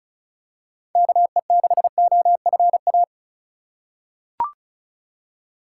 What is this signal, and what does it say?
Morse code, heard as a keyed tone, about 34 wpm, 700 Hz. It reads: KE6OFA
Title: USA Call Signs - 3x Character Spacing - 35wpm